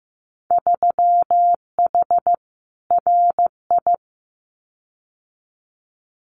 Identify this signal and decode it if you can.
Morse code, heard as a keyed tone, about 15 wpm, 700 Hz. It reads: 3H RI
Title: Winter Field Day Exchanges Encore 15wpm